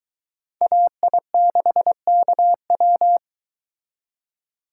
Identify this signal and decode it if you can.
Morse code, heard as a keyed tone, about 23 wpm, 700 Hz. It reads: AI6KW